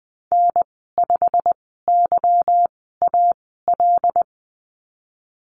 Morse code, heard as a keyed tone, about 20 wpm, spaced as for 14 wpm, 700 Hz. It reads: N5YAL
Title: 161 Lesson 26g - Letter g - Call Signs 20wpm